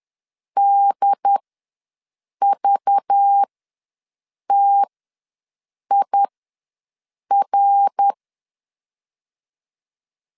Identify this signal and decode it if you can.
Morse code, heard as a keyed tone, about 10 wpm, spaced as for 6 wpm, 800 Hz. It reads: DVTIR
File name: LEZIONI / 5 LBVC / 5 LBV ws 10wpm 3645